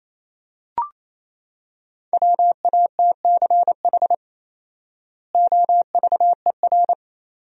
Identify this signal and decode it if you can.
Morse code, heard as a keyed tone, about 28 wpm, 700 Hz. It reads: WATCH OVER
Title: 4x Word-Spacing - Sentences from Top 300 Words - 28wpm